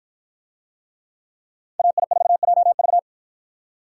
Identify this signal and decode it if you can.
Morse code, heard as a keyed tone, about 53 wpm, 700 Hz. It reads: AI4JV